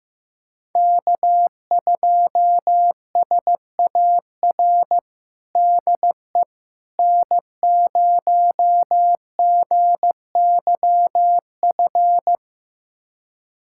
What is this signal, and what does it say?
Morse code, heard as a keyed tone, about 15 wpm, 700 Hz. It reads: K2SAR DE N0GYF
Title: USA Callsign DE Callsign - 5 Characters - 15wpm